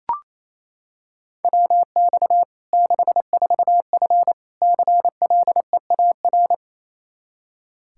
Morse code, heard as a keyed tone, about 28 wpm, 700 Hz. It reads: WX 64F CLEAR
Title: K1USN SST QSO Practice 28wpm